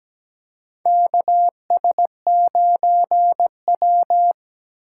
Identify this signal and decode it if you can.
Morse code, heard as a keyed tone, about 17 wpm, 700 Hz. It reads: KS9W
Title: K1USN SST QSO Practice - 17wpm